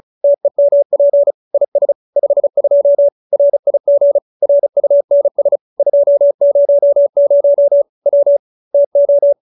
Morse code, heard as a keyed tone, about 34 wpm, 570 Hz. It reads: TEMP IS 52 RIG RUNS 200 W TO